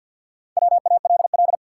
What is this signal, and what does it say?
Morse code, heard as a keyed tone, about 53 wpm, 700 Hz. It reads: WALL
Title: Sentences from Top 1100 Words - New Word Drill Alt - 50wpm